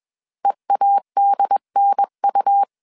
Morse code, heard as a keyed tone, about 21 wpm, 800 Hz. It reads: EADNU